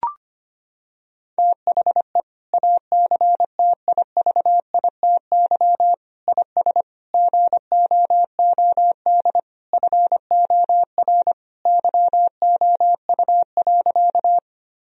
Morse code, heard as a keyed tone, about 25 wpm, 700 Hz. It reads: THE ACTIVITY IS GOOD FOR YOU.
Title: Sentences from Top 1000 Words - Review Part 4 - 25wpm